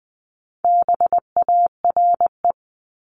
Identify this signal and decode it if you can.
Morse code, heard as a keyed tone, about 20 wpm, 700 Hz. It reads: BARE
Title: 260 Lesson 42e - Words 20-20wpm